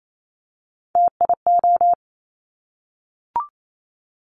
Morse code, heard as a keyed tone, about 28 wpm, 700 Hz. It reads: TIO
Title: Top Three Letter Combinations in Words 28wpm